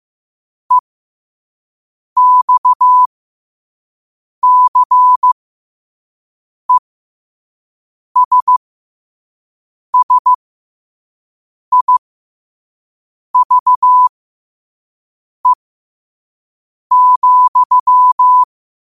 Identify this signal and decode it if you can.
Morse code, heard as a keyed tone, about 15 wpm, spaced as for 5 wpm, 1000 Hz. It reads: EXCESSIVE,
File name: qotdfiles / 5wpm